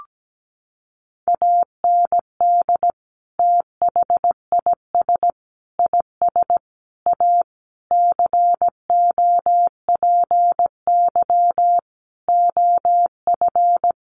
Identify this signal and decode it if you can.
Morse code, heard as a keyed tone, about 17 wpm, 700 Hz. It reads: AND THIS IS A COPY OF
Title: Sentences from Top 1400 Words 17wpm